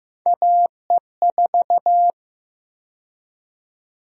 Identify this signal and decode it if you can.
Morse code, heard as a keyed tone, about 15 wpm, 700 Hz. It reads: AE4
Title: USA Call Sign Prefixes - 15wpm